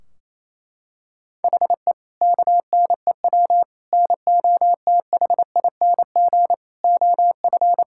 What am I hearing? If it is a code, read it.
Morse code, heard as a keyed tone, about 27 wpm, 700 Hz. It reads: HE KNEW NOTHING OF
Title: Sentences from Top 1600 Words 28wpm